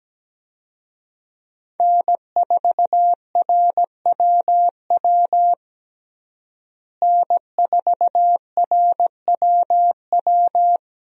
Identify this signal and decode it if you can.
Morse code, heard as a keyed tone, about 17 wpm, 700 Hz. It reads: N4RWW N4RWW
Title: USA Call Signs - 5 Characters - Repeated 2x - 17wpm